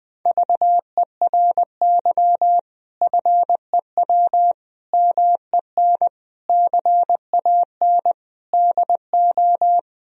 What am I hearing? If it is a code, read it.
Morse code, heard as a keyed tone, about 20 wpm, 700 Hz. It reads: VERY FEW MEN CAN DO
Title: Sentences from Top 1000 Words - Review Part 1 - 20wpm